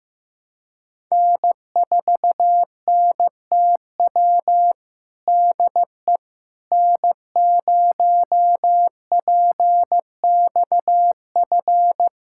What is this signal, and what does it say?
Morse code, heard as a keyed tone, about 15 wpm, 700 Hz. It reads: N4NTW DE N0PXF
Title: USA Callsign DE Callsign - 5 Characters - 15wpm